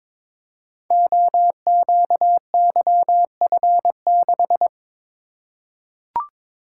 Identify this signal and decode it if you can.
Morse code, heard as a keyed tone, about 22 wpm, 700 Hz. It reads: OQYF6
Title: Code Groups - 5 Letter:Numbers - 22wpm